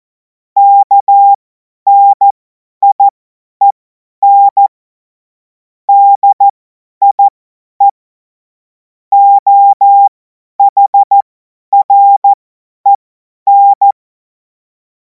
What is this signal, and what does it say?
Morse code, heard as a keyed tone, about 14 wpm, spaced as for 10 wpm, 800 Hz. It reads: KNIEN DIE OHREN
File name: Morsekurs / Witze / Witz 02. Juni 2026 50 F70